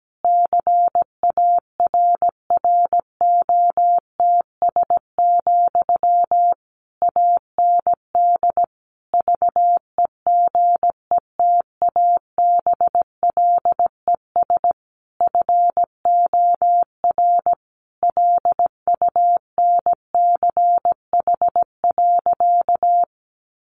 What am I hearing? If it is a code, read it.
Morse code, heard as a keyed tone, about 17 wpm, 700 Hz. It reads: CARROTS, AND VEGETABLES FOR LUNCH.
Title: Easy Phrases 17wpm